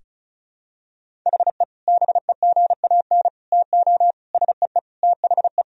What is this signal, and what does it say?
Morse code, heard as a keyed tone, about 34 wpm, 700 Hz. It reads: HE BEGAN TO SEE THE
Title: Sentences from Top 1000 Words - Review Part 1 - 35wpm